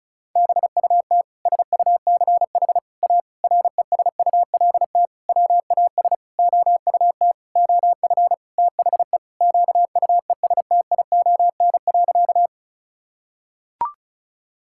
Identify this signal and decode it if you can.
Morse code, heard as a keyed tone, about 34 wpm, 700 Hz. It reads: BUT SUCH A RESULT WAS OUT OF THE QUESTION.
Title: Sentences from Top 1000 Words - Review Part 5 - 35wpm